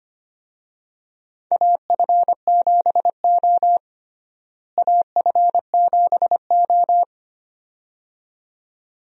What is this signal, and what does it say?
Morse code, heard as a keyed tone, about 25 wpm, 700 Hz. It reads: AF7O AF7O
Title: USA Call Signs - 4 Characters - Repeated 2x - 25wpm